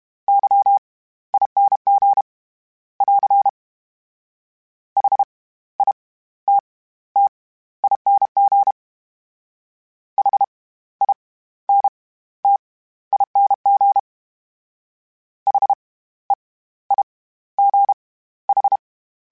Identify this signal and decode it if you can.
Morse code, heard as a keyed tone, about 32 wpm, spaced as for 16 wpm, 800 Hz. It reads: YING <AR> HITTING HINTING HEIGH
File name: Special INGs / Special INGs-mp3s / SlowToFaster ing @30